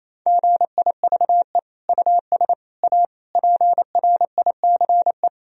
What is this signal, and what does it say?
Morse code, heard as a keyed tone, about 28 wpm, 700 Hz. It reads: GIVE US A PRICE
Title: Sentences from Top 1000 Words - Review Part 4 - 28wpm